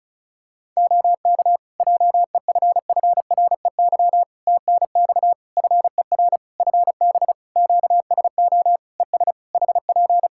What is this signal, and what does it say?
Morse code, heard as a keyed tone, about 34 wpm, 700 Hz. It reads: OK JEFFREY TNX FER FB QSO ES HP
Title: Practice QSO - Exchange 4 - Ending - 35wpm